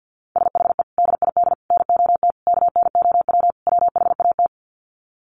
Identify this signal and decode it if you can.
Morse code, heard as a keyed tone, about 49 wpm, 700 Hz. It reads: SHE DID NOT KNOW WHAT
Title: Sets of 5 Words Encore2 50wpm